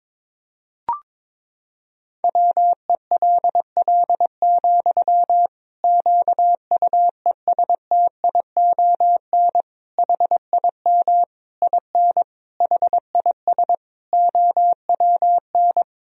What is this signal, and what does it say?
Morse code, heard as a keyed tone, about 22 wpm, 700 Hz. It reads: WELL, QUESTION HIM IN HIS OWN
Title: Sentences from Top 1000 Words - Review Part 3 - 22wpm